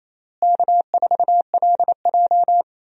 Morse code, heard as a keyed tone, about 28 wpm, 700 Hz. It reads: K4LJ
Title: USA Call Signs - 4 Characters - Repeated 3x - 28wpm